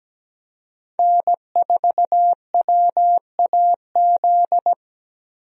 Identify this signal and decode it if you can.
Morse code, heard as a keyed tone, about 17 wpm, 700 Hz. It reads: N4WAZ
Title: USA Call Signs - 5 Characters - 17wpm